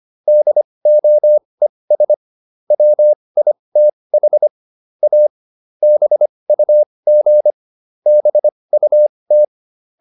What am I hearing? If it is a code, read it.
Morse code, heard as a keyed tone, about 25 wpm, spaced as for 20 wpm, 600 Hz. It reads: DOES WITH A BUG BUT
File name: SS215 20